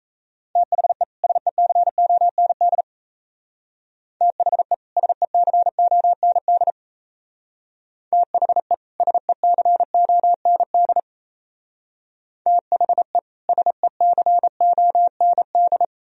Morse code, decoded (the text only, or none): THE SECOND THE SECOND THE SECOND THE SECOND